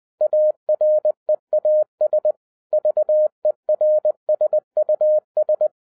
Morse code, heard as a keyed tone, about 20 wpm, 610 Hz. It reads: AREAS VERSUS